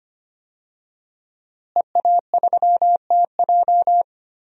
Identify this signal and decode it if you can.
Morse code, heard as a keyed tone, about 25 wpm, 700 Hz. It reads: EA3TJ
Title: DXCC-100 Call Signs - 5 Characters - Repeated 3x - 25wpm